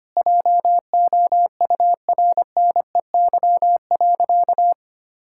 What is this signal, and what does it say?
Morse code, heard as a keyed tone, about 25 wpm, 700 Hz. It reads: JOURNEY.
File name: practice / 25wpm / Sentences with 9-Word Phrases - 25wpm